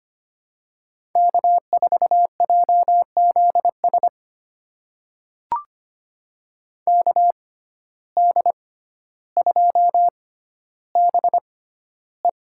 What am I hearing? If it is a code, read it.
Morse code, heard as a keyed tone, about 25 wpm, spaced as for 13 wpm, 700 Hz. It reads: K4JZS KD2BE
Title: USA Call Signs - 5 Characters - 6x Character Spacing - 25wpm